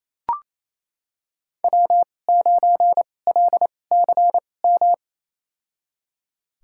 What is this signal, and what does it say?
Morse code, heard as a keyed tone, about 28 wpm, spaced as for 20 wpm, 700 Hz. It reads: W9LCM
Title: USA Call Signs - 2x Character Spacing - 28wpm